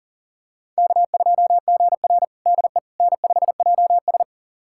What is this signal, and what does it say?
Morse code, heard as a keyed tone, about 40 wpm, 700 Hz. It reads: K2GR DE N5JS
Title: USA Callsign DE Callsign - 4 Characters - 40wpm